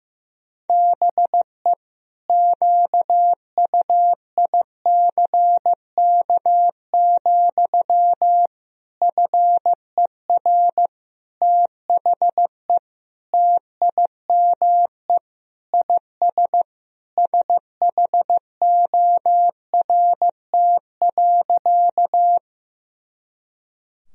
Morse code, heard as a keyed tone, about 15 wpm, 700 Hz. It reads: BE QUICK, FER THE TIME IS SHORT.
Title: Sentences from Top 500 Words and Abbreviations 15wpm